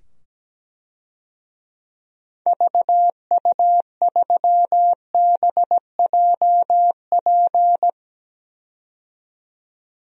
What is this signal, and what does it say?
Morse code, heard as a keyed tone, about 17 wpm, 700 Hz. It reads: VU3BJP
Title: DXCC-100 Call Signs - 6 Characters - 17wpm